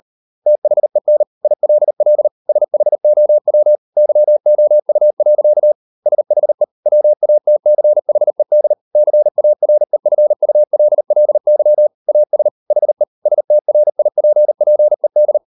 Morse code, heard as a keyed tone, about 38 wpm, 600 Hz. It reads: THEN ILL SHOW YOU. SHE WATCHED CAREFULLY AS HE STRIPPED